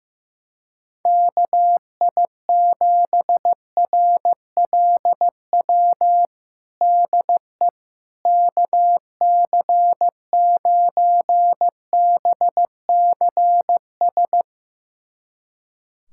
Morse code, heard as a keyed tone, about 15 wpm, 700 Hz. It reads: KI7RLW DE KC9BCS
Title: USA Callsign DE Callsign - 6 Characters - 15wpm